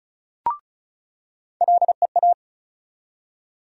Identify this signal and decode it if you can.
Morse code, heard as a keyed tone, about 36 wpm, 700 Hz. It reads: LEA
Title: USA Call Sign Postfixes - 3 Letters 35wpm